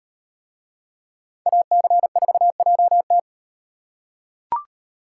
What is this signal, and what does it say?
Morse code, heard as a keyed tone, about 37 wpm, 700 Hz. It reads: AC4JT